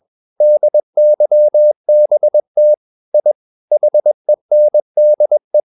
Morse code, heard as a keyed tone, about 21 wpm, 600 Hz. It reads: DYBT I HENDE
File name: Ferie QRS 21wpm